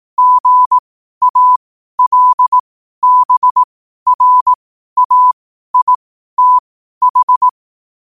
Morse code, heard as a keyed tone, about 18 wpm, spaced as for 12 wpm, 1000 Hz. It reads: GALBRAITH
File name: qotdfiles / 12wpm